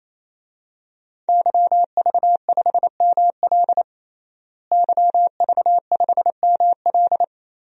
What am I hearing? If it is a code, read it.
Morse code, heard as a keyed tone, about 28 wpm, 700 Hz. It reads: YV5ML YV5ML
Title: DXCC-100 Call Signs - 5 Characters - Repeated 3x - 28wpm